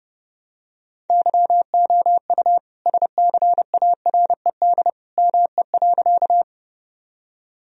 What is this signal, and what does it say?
Morse code, heard as a keyed tone, about 30 wpm, 700 Hz. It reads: YOU SCARED ME.
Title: Sentences from Top 1000 Words - Review Part 4 - 30wpm